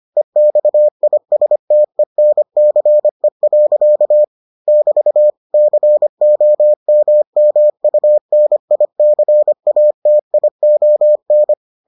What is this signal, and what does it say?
Morse code, heard as a keyed tone, about 25 wpm, 600 Hz. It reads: EXISTENCE. <BT> COMMUNICATION